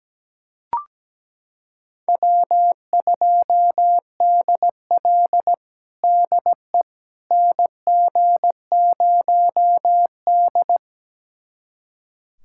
Morse code, heard as a keyed tone, about 17 wpm, 700 Hz. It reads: W2DL DE NG0D
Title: USA Callsign DE Callsign - 4 Characters - 17wpm